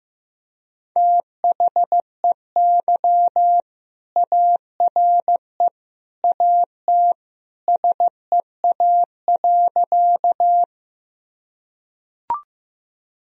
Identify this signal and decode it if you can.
Morse code, heard as a keyed tone, about 15 wpm, 700 Hz. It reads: THEY ARE AT SEA.
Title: Sentences from Top 1000 Words - Review Part 1 - 15wpm